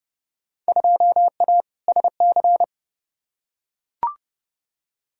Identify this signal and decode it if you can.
Morse code, heard as a keyed tone, about 29 wpm, 700 Hz. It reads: 2A SC
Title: Field Day Exchanges - 30wpm